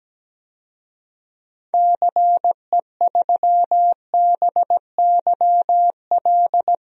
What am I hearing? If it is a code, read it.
Morse code, heard as a keyed tone, about 17 wpm, 700 Hz. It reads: CE3BYL
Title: DXCC-100 Call Signs - 6 Characters - Repeated 3x - 17wpm